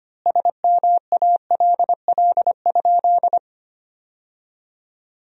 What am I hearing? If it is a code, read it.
Morse code, heard as a keyed tone, about 25 wpm, 700 Hz. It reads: SMALL?
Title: 3x Word-Spacing - Sentences from Top 200 Words - 25wpm